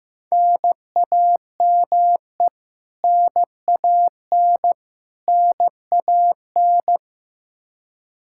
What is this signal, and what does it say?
Morse code, heard as a keyed tone, about 15 wpm, 700 Hz. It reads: NAME NAN NAN
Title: Practice QSO - Exchange 1 - Beginning - 15wpm